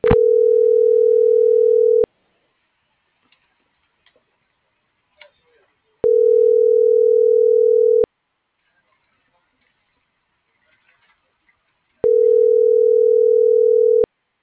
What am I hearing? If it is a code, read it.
Ringback tone (North America), 440 + 480 Hz.
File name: CENTRAL TERRESTRE / out erick.sanjuan